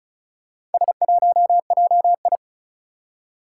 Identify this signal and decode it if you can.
Morse code, heard as a keyed tone, about 36 wpm, 690 Hz. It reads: S1JI
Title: Four Challenging Letter-Number Combinations 35wpm